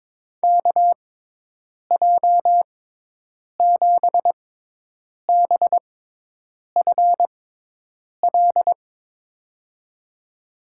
Morse code, heard as a keyed tone, about 22 wpm, spaced as for 8 wpm, 700 Hz. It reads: KJ7BFL
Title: USA Call Signs - 6 Characters - 6x Character Spacing - 22wpm